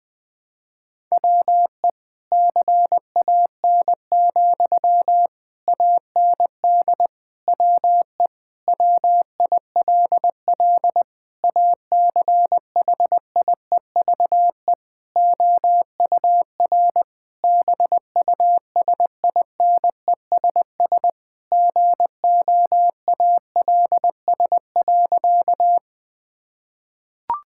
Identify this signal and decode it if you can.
Morse code, heard as a keyed tone, about 20 wpm, 700 Hz. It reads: WE CAN, AND WE WILL ACHIEVE OUR BUSINESS GOALS.
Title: Sentences from Top 1000 Words - Review Part 6 - 20wpm